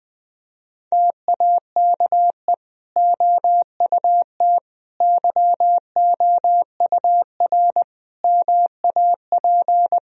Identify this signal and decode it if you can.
Morse code, heard as a keyed tone, about 20 wpm, 700 Hz. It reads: TAKE OUT YOUR MAP
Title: Sentences from Top 1000 Words - Review Part 2 - 20wpm